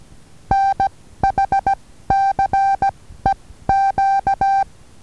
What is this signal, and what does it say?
Morse code, heard as a keyed tone, about 19 wpm, spaced as for 14 wpm, 780 Hz. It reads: NHCEQ
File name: Taviro HA6VA B